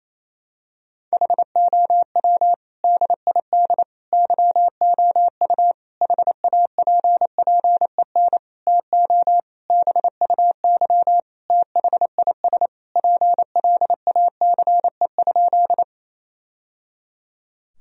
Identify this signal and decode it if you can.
Morse code, heard as a keyed tone, about 28 wpm, 700 Hz. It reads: HOW DID YOU HAPPEN TO BUY THIS PLACE?
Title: Sentences from Top 1000 Words - Review Part 4 - 28wpm